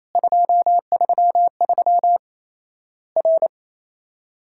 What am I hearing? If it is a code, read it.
Morse code, heard as a keyed tone, about 28 wpm, 700 Hz. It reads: 233 R
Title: Worked All Europe DX Contest - QTC Exchange - 28wpm